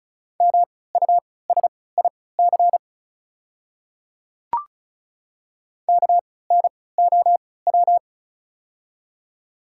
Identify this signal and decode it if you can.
Morse code, heard as a keyed tone, about 36 wpm, spaced as for 20 wpm, 700 Hz. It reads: MUSIC KNOW
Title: 3x Character Spacing - Top 300 Words - 35wpm